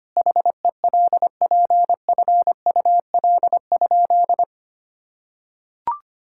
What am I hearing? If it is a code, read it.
Morse code, heard as a keyed tone, about 25 wpm, 700 Hz. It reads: HELPFUL?
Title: Sentences from Top 1000 Words - Review Part 6 - 25wpm